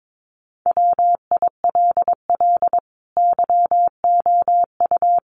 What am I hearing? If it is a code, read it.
Morse code, heard as a keyed tone, about 22 wpm, 700 Hz. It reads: WILL YOU
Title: Sentences from Top 900 Words 22wpm